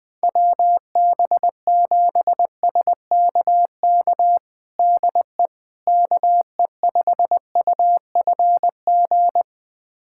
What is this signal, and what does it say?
Morse code, heard as a keyed tone, about 20 wpm, 700 Hz. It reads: WB7SKK DE KE5UFG
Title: USA Callsign DE Callsign - 6 Characters - 20wpm